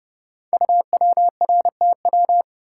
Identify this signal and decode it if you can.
Morse code, heard as a keyed tone, about 30 wpm, 700 Hz. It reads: UWRTW
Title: Code Groups - 5 Letters - 30wpm